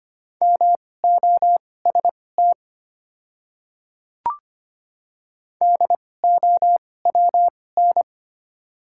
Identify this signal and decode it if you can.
Morse code, heard as a keyed tone, about 25 wpm, spaced as for 18 wpm, 690 Hz. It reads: MOST DOWN
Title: 2x Character Spacing - Top 100 Words - 25wpm